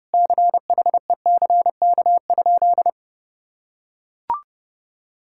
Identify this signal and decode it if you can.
Morse code, heard as a keyed tone, about 30 wpm, 700 Hz. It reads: CHECK?
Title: Sentences from Top 1000 Words - Review Part 6 - 30wpm